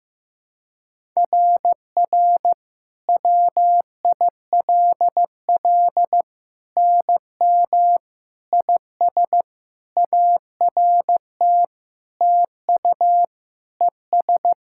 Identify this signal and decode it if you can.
Morse code, heard as a keyed tone, about 15 wpm, 700 Hz. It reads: RR WILL NM IS ART TU ES
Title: SOTA QSO Practice 15wpm